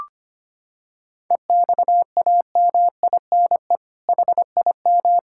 Morse code, heard as a keyed tone, about 25 wpm, 700 Hz. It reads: EXAMINE HIM
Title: Sentences from Top 1600 Words 25wpm